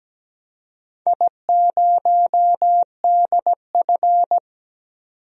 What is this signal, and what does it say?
Morse code, heard as a keyed tone, about 17 wpm, 700 Hz. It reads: I0DF
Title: DXCC-100 Call Signs - 4 Characters - 17wpm